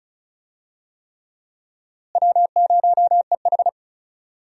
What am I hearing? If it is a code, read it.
Morse code, heard as a keyed tone, about 34 wpm, 700 Hz. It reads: W0EH
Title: USA Call Signs - 4 Characters - Repeated 3x - 35wpm